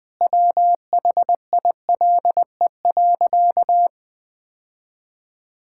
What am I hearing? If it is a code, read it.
Morse code, heard as a keyed tone, about 21 wpm, 700 Hz. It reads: WHILE.
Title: Sentences with 3-Word Phrases - 20wpm